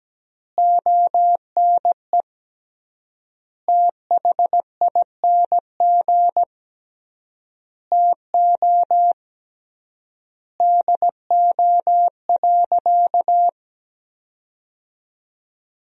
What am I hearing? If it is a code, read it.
Morse code, heard as a keyed tone, about 17 wpm, 700 Hz. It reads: ONE THING TO DO.
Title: 3x Word-Spacing - Sentences from Top 100 Words - 17wpm